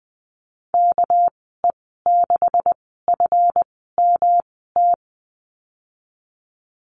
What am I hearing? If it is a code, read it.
Morse code, heard as a keyed tone, about 20 wpm, spaced as for 15 wpm, 700 Hz. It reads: KE6FMT
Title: 200 Lesson 32g - Letter k - Call Signs 20wpm